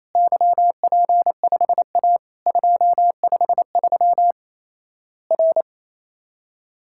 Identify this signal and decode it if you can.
Morse code, heard as a keyed tone, about 28 wpm, 700 Hz. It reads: YP5A 253 R
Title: Worked All Europe DX Contest - QTC Exchange - 28wpm